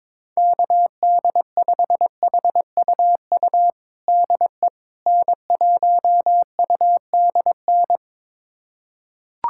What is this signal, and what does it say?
Morse code, heard as a keyed tone, about 21 wpm, 700 Hz. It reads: KD5HUU DE N1UDN
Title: USA Callsign DE Callsign 22wpm